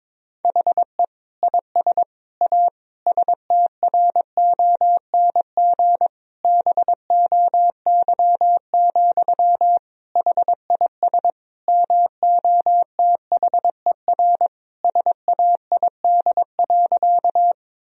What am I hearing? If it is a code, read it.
Morse code, heard as a keyed tone, about 22 wpm, 700 Hz. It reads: HE IS A STRONG BOY, HIS MOTHER SAID.
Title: Sentences from Top 1000 Words - Review Part 3 - 22wpm